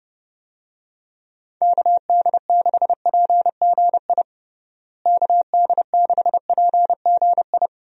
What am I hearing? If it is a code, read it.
Morse code, heard as a keyed tone, about 30 wpm, 700 Hz. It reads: KD6PGI KD6PGI
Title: USA Call Signs - 6 Characters - Repeated 3x - 30wpm